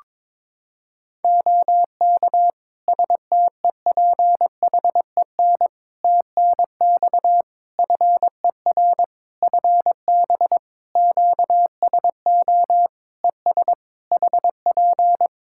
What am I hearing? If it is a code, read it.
Morse code, heard as a keyed tone, about 22 wpm, 700 Hz. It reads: OK STEPHEN TNX FER FB QSO ES HP
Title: Practice QSO - Exchange 4 - Ending - 22wpm